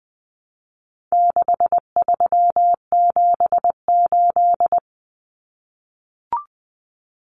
Morse code, heard as a keyed tone, about 20 wpm, 700 Hz. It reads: 6378
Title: Numbers - 4-Digit 20wpm